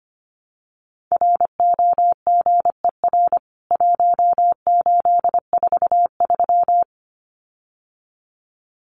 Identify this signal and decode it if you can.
Morse code, heard as a keyed tone, about 25 wpm, 700 Hz. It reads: ROGER 1843
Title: CWT Exchange 25wpm